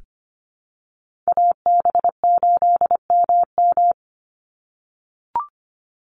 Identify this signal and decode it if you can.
Morse code, heard as a keyed tone, about 25 wpm, 700 Hz. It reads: AB8MM
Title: USA Call Signs 25wpm